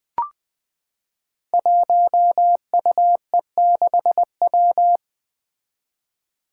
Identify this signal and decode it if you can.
Morse code, heard as a keyed tone, about 20 wpm, 700 Hz. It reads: 1UE6W
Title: Five Challenging Letter-Number Combinations 20wpm